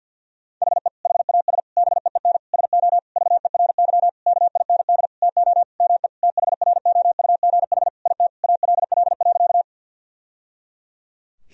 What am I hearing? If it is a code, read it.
Morse code, heard as a keyed tone, about 50 wpm, 700 Hz. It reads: HE HAS BEEN SO VERY KIND TO ME THROUGH IT ALL.
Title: Sentences from Top 1000 Words - Review Part 1 - 50wpm